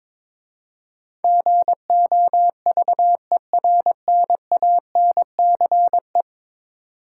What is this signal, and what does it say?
Morse code, heard as a keyed tone, about 22 wpm, 700 Hz. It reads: GOVERNANCE
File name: practice / 22wpm / 10 Letter Words - First 3 Letters Given - 22wpm